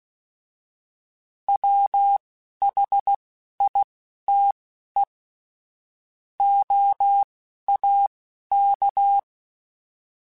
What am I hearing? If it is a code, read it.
Morse code, heard as a keyed tone, about 16 wpm, spaced as for 11 wpm, 800 Hz. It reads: WHITE OAK